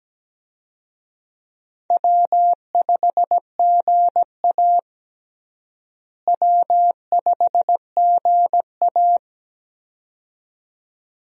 Morse code, heard as a keyed tone, about 17 wpm, 700 Hz. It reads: W5GA W5GA
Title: USA Call Signs - 4 Characters - Repeated 2x - 17wpm